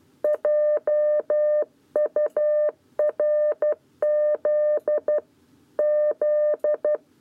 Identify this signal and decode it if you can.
Morse code, heard as a keyed tone, about 12 wpm, 600 Hz. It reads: JURZ Z
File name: 7b lezione cw 1